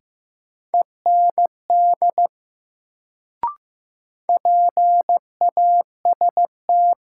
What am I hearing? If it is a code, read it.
Morse code, heard as a keyed tone, about 15 wpm, 700 Hz. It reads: END PAST
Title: Sets of 2 Words - New Word Drill - 15wpm